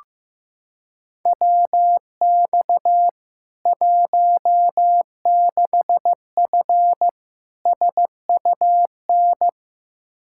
Morse code, heard as a keyed tone, about 15 wpm, 700 Hz. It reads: WX 16F SUN
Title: K1USN SST QSO Practice 15wpm